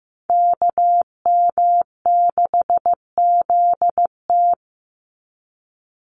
Morse code, heard as a keyed tone, about 15 wpm, 700 Hz. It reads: KM6ZT
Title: Call Signs Repeated 2x - 15wpm